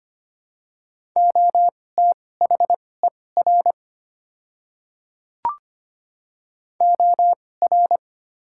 Morse code, heard as a keyed tone, about 25 wpm, spaced as for 18 wpm, 690 Hz. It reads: OTHER OR
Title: 2x Character Spacing - Top 100 Words - 25wpm